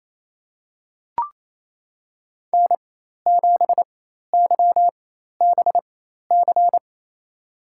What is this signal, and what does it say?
Morse code, heard as a keyed tone, about 28 wpm, spaced as for 13 wpm, 700 Hz. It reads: N7YBC
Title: USA Call Signs - 5 Characters - 4x Character Spacing - 28wpm